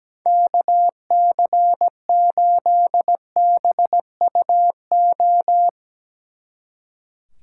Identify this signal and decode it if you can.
Morse code, heard as a keyed tone, about 18 wpm, 700 Hz. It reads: KC8BUO
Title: Speed Racing - USA Call Signs 17wpm